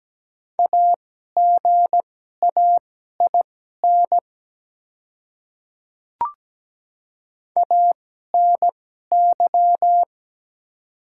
Morse code, heard as a keyed tone, about 17 wpm, spaced as for 12 wpm, 700 Hz. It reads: AGAIN ANY
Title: 2x Character Spacing - Top 300 Words - 17wpm